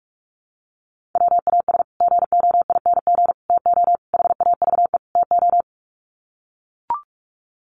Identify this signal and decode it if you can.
Morse code, heard as a keyed tone, about 44 wpm, 700 Hz. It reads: WAS GOING TO HAVE TO
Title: Sets of 5 Words Encore 45wpm